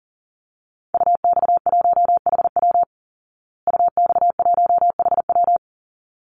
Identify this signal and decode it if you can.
Morse code, heard as a keyed tone, about 40 wpm, 700 Hz. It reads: UX1HW UX1HW
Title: Call Signs Repeated 3x - 40wpm